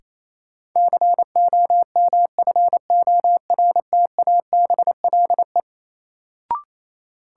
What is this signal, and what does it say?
Morse code, heard as a keyed tone, about 28 wpm, 700 Hz. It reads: COMFORTABLE
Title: Sentences from Top 1000 Words - Review Part 7 - New Word Drill - 28wpm